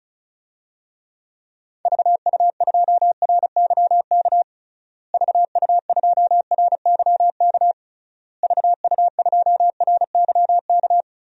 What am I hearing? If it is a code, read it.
Morse code, heard as a keyed tone, about 36 wpm, 700 Hz. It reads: VU2RYK VU2RYK VU2RYK
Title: DXCC-100 Call Signs - 6 Characters - Repeated 3x - 35wpm